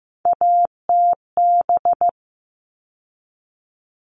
Morse code, heard as a keyed tone, about 15 wpm, 700 Hz. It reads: ATB
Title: All Cut Numbers - 3-Digit 15wpm